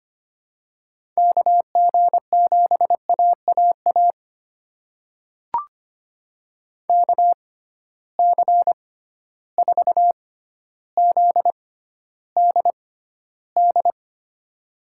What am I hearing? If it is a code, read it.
Morse code, heard as a keyed tone, about 25 wpm, spaced as for 13 wpm, 700 Hz. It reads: KG7AAA KC4ZDD
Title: USA Call Signs - 6 Characters - 6x Character Spacing - 25wpm